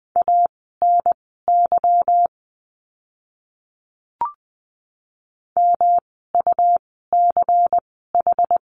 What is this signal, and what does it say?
Morse code, heard as a keyed tone, about 20 wpm, spaced as for 15 wpm, 700 Hz. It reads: ANY MUCH
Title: 230 Lesson 37f - Letter x - Words 20wpm